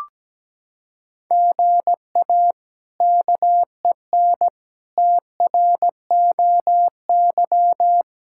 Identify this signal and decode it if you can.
Morse code, heard as a keyed tone, about 17 wpm, 700 Hz. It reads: GA KEN TROY
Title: K1USN SST QSO Practice - 17wpm